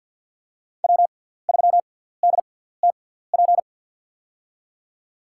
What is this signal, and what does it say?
Morse code, heard as a keyed tone, about 50 wpm, spaced as for 17 wpm, 700 Hz. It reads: W3DTP
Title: USA Call Signs - 5 Characters - 6x Character Spacing - 50wpm